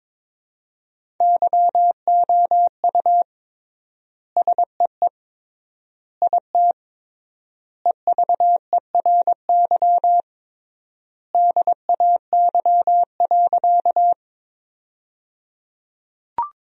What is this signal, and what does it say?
Morse code, heard as a keyed tone, about 22 wpm, 700 Hz. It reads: YOU SEE IT EVERY DAY.
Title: 3x Word-Spacing - Sentences from Top 200 Words - 22wpm